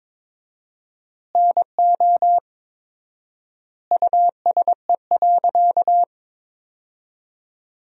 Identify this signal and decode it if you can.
Morse code, heard as a keyed tone, about 22 wpm, 700 Hz. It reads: NO USE.
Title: 4x Word-Spacing - Sentences from Top 100 Words - 22wpm